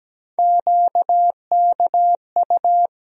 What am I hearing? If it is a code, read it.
Morse code, heard as a keyed tone, about 17 wpm, 700 Hz. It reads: QKU
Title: USA Call Sign Postfixes - 3 Letters 17wpm